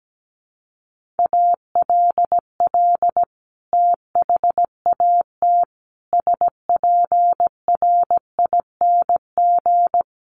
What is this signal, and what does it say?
Morse code, heard as a keyed tone, about 17 wpm, 700 Hz. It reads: ALL THAT SPRING
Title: Sentences from Top 1200 Words 17wpm